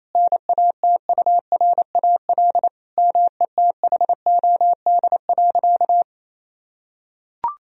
Morse code, heard as a keyed tone, about 27 wpm, 700 Hz. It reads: NATURAL METHOD.
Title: Sentences from Top 1000 Words - Review Part 4 - 28wpm